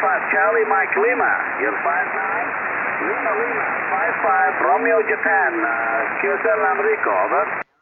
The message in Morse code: none